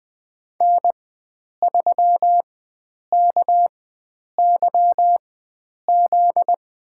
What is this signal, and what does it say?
Morse code, heard as a keyed tone, about 20 wpm, spaced as for 9 wpm, 700 Hz. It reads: N3KYZ
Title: USA Call Signs - 5 Characters - 4x Character Spacing - 20wpm